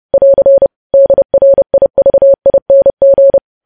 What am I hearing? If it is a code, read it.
Morse code, heard as a keyed tone, about 30 wpm, 550 Hz. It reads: <AR> DRIVING
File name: podcast / News headlines-1x-30x30